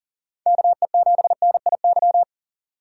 Morse code, heard as a keyed tone, about 49 wpm, 700 Hz. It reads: KE7NIY
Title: Speed Racing - USA 2x3 Call Signs 40wpm